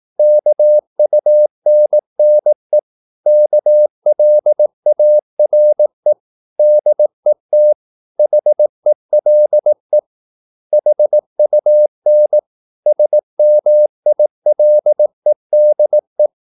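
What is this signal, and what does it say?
Morse code, heard as a keyed tone, about 18 wpm, 600 Hz. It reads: KUNNE KLARE DET HELE HUN SMILEDE
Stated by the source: Slow (QRS) Morse practice at 18 wpm, CW med Gnister.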